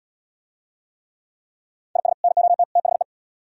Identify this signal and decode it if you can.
Morse code, heard as a keyed tone, about 71 wpm, 700 Hz. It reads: IS NOT THE